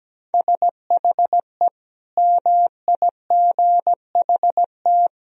Morse code, heard as a keyed tone, about 17 wpm, 700 Hz. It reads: SHE MIGHT
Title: Sentences from Top 1000 Words - Review Part 4 - 17wpm